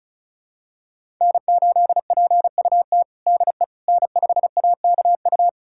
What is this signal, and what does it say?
Morse code, heard as a keyed tone, about 34 wpm, 700 Hz. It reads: N8PUT DE N5AKU
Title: USA Callsign DE Callsign - 5 Characters - 35wpm